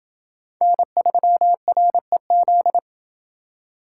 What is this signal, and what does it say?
Morse code, heard as a keyed tone, about 28 wpm, 700 Hz. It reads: N3REZ